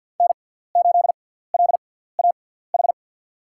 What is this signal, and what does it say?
Morse code, heard as a keyed tone, about 52 wpm, spaced as for 17 wpm, 700 Hz. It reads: N8LAH
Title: USA Call Signs - 5 Characters - 6x Character Spacing - 50wpm